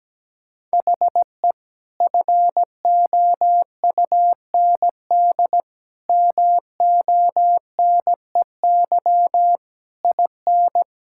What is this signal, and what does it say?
Morse code, heard as a keyed tone, about 17 wpm, 700 Hz. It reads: HE FOUND MONEY IN
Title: Sentences from Top 1000 Words - Review Part 6 - 17wpm